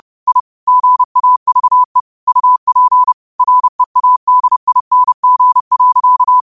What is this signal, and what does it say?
Morse code, heard as a keyed tone, about 30 wpm, 1000 Hz. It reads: I GAVE UP READING.
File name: qotdfiles / 30wpm